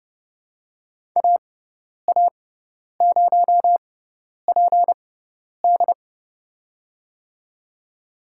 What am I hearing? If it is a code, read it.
Morse code, heard as a keyed tone, about 32 wpm, spaced as for 10 wpm, 700 Hz. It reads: AA0PD
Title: USA Call Signs - 5 Characters - 6x Character Spacing - 30wpm